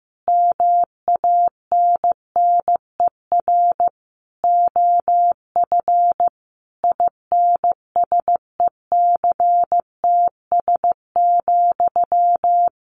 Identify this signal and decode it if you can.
Morse code, heard as a keyed tone, about 15 wpm, 700 Hz. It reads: MANNER OF INSECTS,
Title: A Sound Of Thunder 15wpm